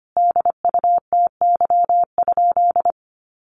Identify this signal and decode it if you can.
Morse code, heard as a keyed tone, about 25 wpm, 700 Hz. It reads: DUTY?
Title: Sentences from Top 1500 Words 25wpm